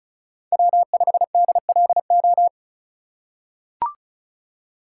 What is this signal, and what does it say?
Morse code, heard as a keyed tone, about 34 wpm, 700 Hz. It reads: W5DLO
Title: USA Call Signs - 5 Characters - 3x Character Spacing - 35wpm